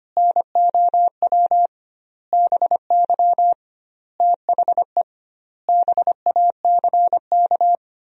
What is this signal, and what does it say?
Morse code, heard as a keyed tone, about 25 wpm, 700 Hz. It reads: NOW BY THE BACK
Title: 2x Word-Spacing - Sentences from Top 300 Words - 25wpm